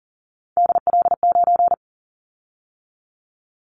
Morse code, heard as a keyed tone, about 40 wpm, 700 Hz. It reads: DL9
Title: Callsign Prefixes for Top 50 DXCC - 40wpm